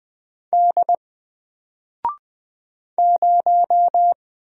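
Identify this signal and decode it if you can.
Morse code, heard as a keyed tone, about 20 wpm, 700 Hz. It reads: D 0
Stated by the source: Single letter and number Morse drill at 20 wpm, from N4LCD.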